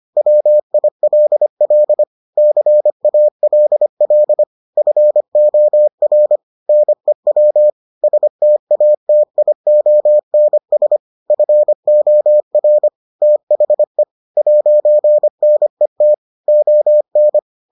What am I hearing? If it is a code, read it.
Morse code, heard as a keyed tone, about 25 wpm, 600 Hz. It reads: WILL CALL FOR NEW STATIONS FOR THE 'NET ON